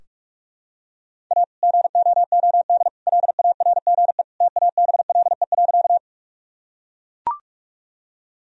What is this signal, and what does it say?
Morse code, heard as a keyed tone, about 42 wpm, 700 Hz. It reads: A GOOD LARGE TABLE.
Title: Sentences from Top 1000 Words - Review Part 2 - 45wpm